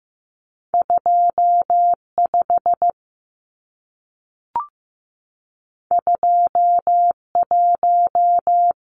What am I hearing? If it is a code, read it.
Morse code, heard as a keyed tone, about 15 wpm, 700 Hz. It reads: 25 21
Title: Numbers - 2-Digit 15wpm